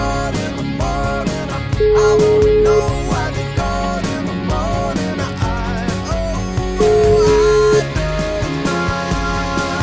Ringback tone (Europe, 425 Hz), with noise.